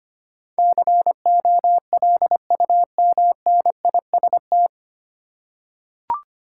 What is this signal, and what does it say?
Morse code, heard as a keyed tone, about 25 wpm, 700 Hz. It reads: COLUMNIST
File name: practice / 25wpm / 9 Letter Words - First 3 Letters Given - 25wpm